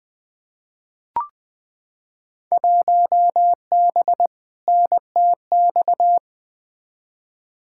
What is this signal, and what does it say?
Morse code, heard as a keyed tone, about 20 wpm, 700 Hz. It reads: 1B NTX
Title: Field Day Exchanges - 20wpm